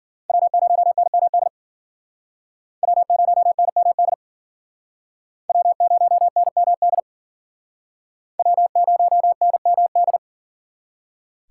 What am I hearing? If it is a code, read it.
Morse code, heard as a keyed tone, about 53 wpm, 700 Hz. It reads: W0NMD W0NMD W0NMD W0NMD